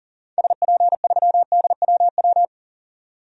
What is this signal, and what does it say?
Morse code, heard as a keyed tone, about 40 wpm, 700 Hz. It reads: SP3DWW
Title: DXCC-100 Call Signs - 6 Characters - 40wpm